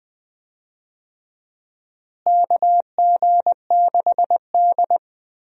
Morse code, heard as a keyed tone, about 20 wpm, 700 Hz. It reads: KG6D
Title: USA Call Signs - 4 Characters - Repeated 2x - 20wpm